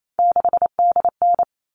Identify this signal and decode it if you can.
Morse code, heard as a keyed tone, about 28 wpm, 700 Hz. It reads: 6DN
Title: All Cut Numbers - 3-Digit 28wpm